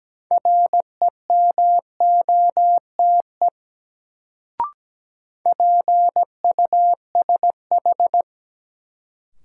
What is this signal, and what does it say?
Morse code, heard as a keyed tone, about 17 wpm, 700 Hz. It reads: REMOTE PUSH
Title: Sentences from Top 1300 Words - New Word Drill - 17wpm